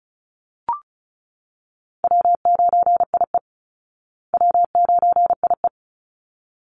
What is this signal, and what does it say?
Morse code, heard as a keyed tone, about 34 wpm, 700 Hz. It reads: W9IE W9IE
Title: Easy Call Signs Repeated 2x - 35wpm